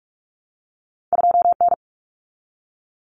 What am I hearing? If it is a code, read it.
Morse code, heard as a keyed tone, about 46 wpm, 700 Hz. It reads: 2N
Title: Common Cut Numbers - 2-Digit 45wpm